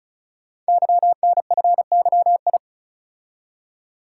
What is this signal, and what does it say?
Morse code, heard as a keyed tone, about 36 wpm, 700 Hz. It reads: YNFYI
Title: Code Groups - 5 Letters - 35wpm